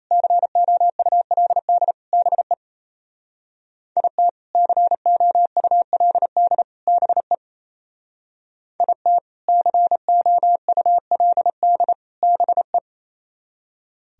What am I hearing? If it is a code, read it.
Morse code, heard as a keyed tone, about 33 wpm, 700 Hz. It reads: COULD BE IT COULD BE IT COULD BE
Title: Speed-Racing - Sets of 3 Words 28wpm